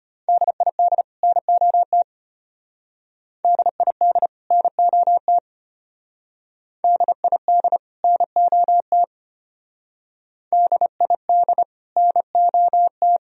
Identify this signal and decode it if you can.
Morse code, heard as a keyed tone, about 33 wpm, 700 Hz. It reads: DID NOT DID NOT DID NOT DID NOT